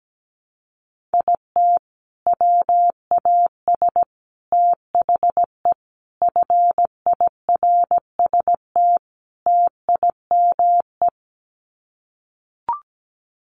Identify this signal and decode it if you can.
Morse code, heard as a keyed tone, about 17 wpm, 700 Hz. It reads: IT WAS THE FIRST TIME
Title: Sets of 5 Words 17wpm